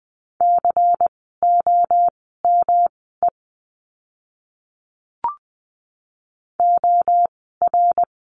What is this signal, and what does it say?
Morse code, heard as a keyed tone, about 20 wpm, spaced as for 14 wpm, 700 Hz. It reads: COME OR
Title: 115 Lesson 19f - Letter w - Words 20wpm